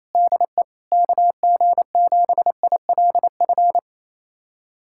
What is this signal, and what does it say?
Morse code, heard as a keyed tone, about 28 wpm, 700 Hz. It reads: DE KG7ILF
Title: USA Callsign DE Callsign 28wpm